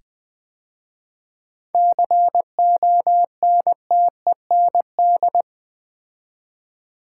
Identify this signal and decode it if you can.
Morse code, heard as a keyed tone, about 20 wpm, 700 Hz. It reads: CONTEND
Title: 7 Letter Words - First 3 Letters Given - 20wpm